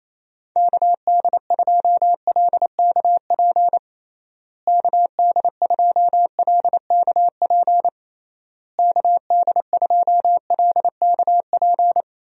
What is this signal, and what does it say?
Morse code, heard as a keyed tone, about 27 wpm, 700 Hz. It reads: KD2LKP KD2LKP KD2LKP
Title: USA Call Signs - 6 Characters - Repeated 3x - 28wpm